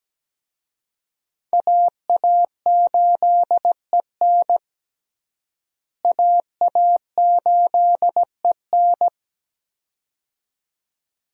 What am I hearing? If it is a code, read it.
Morse code, heard as a keyed tone, about 17 wpm, 700 Hz. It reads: AA8EN AA8EN
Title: USA Call Signs - 5 Characters - Repeated 2x - 17wpm